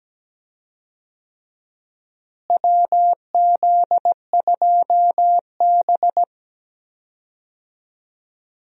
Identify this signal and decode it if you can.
Morse code, heard as a keyed tone, about 17 wpm, 700 Hz. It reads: WZ2B
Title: USA Call Signs - 4 Characters - Repeated 2x - 17wpm